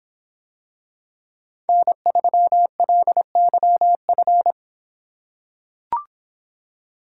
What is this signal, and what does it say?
Morse code, heard as a keyed tone, about 26 wpm, 700 Hz. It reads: N3LYF